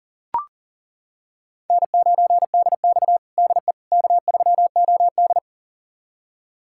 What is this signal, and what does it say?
Morse code, heard as a keyed tone, about 40 wpm, 700 Hz. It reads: N9DX DE K3OD
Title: USA Callsign DE Callsign - 4 Characters - 40wpm